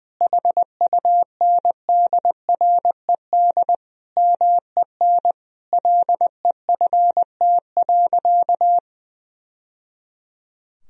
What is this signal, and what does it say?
Morse code, heard as a keyed tone, about 21 wpm, 700 Hz. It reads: HUNDRED MEN LEFT.
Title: Sentences from Top 1000 Words - Review Part 2 - 20wpm